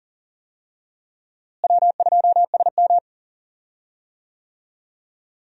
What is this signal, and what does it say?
Morse code, heard as a keyed tone, about 40 wpm, 700 Hz. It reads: W2SM
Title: USA Call Signs - 4 Characters - Repeated 3x - 40wpm